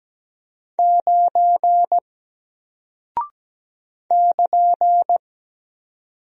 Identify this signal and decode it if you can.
Morse code, heard as a keyed tone, about 17 wpm, 700 Hz. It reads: 9 <KN>
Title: Single Character (Letter:Number:Punctuation:Prosign) Alt 17wpm